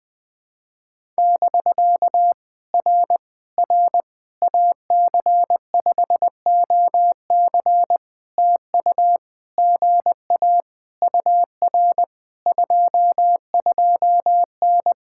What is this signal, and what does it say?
Morse code, heard as a keyed tone, about 20 wpm, 700 Hz. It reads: <BK> R R AC5OC TU GA UR 22N
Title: POTA QSO Practice - 20wpm